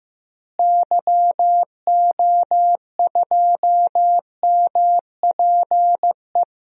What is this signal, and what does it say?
Morse code, heard as a keyed tone, about 15 wpm, 700 Hz. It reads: YO2MPE
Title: DXCC-100 Call Signs - 6 Characters - Repeated 2x - 15wpm